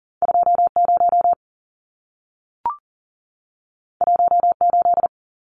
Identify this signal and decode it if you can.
Morse code, heard as a keyed tone, about 40 wpm, 700 Hz. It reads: 20 18
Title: Numbers - 2-Digit 40wpm